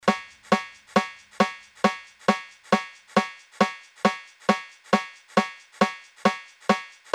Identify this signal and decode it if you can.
There is no code.